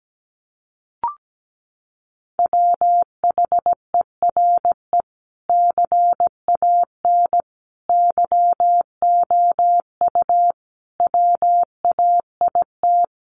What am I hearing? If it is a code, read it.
Morse code, heard as a keyed tone, about 17 wpm, 700 Hz. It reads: WHERE CAN YOU WAIT
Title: Sentences from Top 500 Words 17wpm